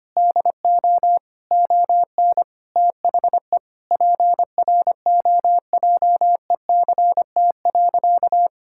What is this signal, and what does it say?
Morse code, heard as a keyed tone, about 25 wpm, 700 Hz. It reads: DO ON THE PROJECT.
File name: practice / 25wpm / Sentences with 9-Word Phrases - 25wpm